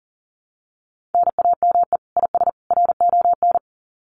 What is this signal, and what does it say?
Morse code, heard as a keyed tone, about 40 wpm, 700 Hz. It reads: NAME IS RON
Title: Common QSO Phrases 40wpm